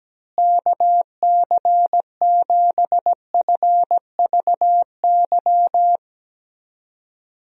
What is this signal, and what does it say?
Morse code, heard as a keyed tone, about 17 wpm, 700 Hz. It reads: KC7FVY
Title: USA Call Signs - 6 Characters - Repeated 2x - 17wpm